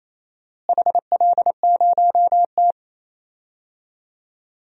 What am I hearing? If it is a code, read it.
Morse code, heard as a keyed tone, about 28 wpm, 700 Hz. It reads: HL0T
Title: DXCC-100 Call Signs - 4 Characters - Repeated 3x - 28wpm